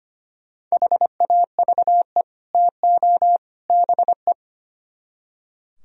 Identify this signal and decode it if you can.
Morse code, heard as a keyed tone, about 33 wpm, 700 Hz. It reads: HAVE TO BE